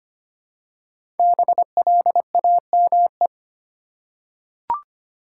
Morse code, heard as a keyed tone, about 25 wpm, 700 Hz. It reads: BLAME
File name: 5 Letter Words - First 3 Letters Given - 25wpm